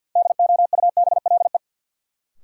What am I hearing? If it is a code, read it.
Morse code, heard as a keyed tone, about 52 wpm, 700 Hz. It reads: DOUBLE
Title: Sentences from Top 1400 Words - New Word Drill - 50wpm